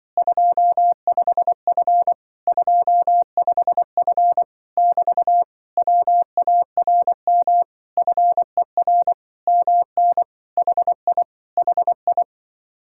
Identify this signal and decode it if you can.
Morse code, heard as a keyed tone, about 24 wpm, 700 Hz. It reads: 25F 25F <BT> WARM FER MN HI HI
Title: Speed-Racing Common QSO Phrases - 20wpm